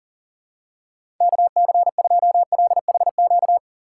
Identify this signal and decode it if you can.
Morse code, heard as a keyed tone, about 39 wpm, 700 Hz. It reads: KC2LHQ
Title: USA Call Signs - 6 Characters - 4x Character Spacing - 40wpm